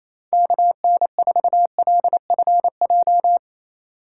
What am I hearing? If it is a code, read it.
Morse code, heard as a keyed tone, about 28 wpm, 700 Hz. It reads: KN4LFJ
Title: USA Call Signs - 2x Character Spacing - 28wpm